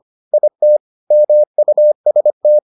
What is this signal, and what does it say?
Morse code, heard as a keyed tone, about 25 wpm, 600 Hz. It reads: IT MUST